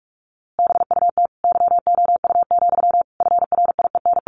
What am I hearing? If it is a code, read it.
Morse code, heard as a keyed tone, about 46 wpm, 700 Hz. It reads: BUT YOU, FRIEN
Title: A Sound Of Thunder 45wpm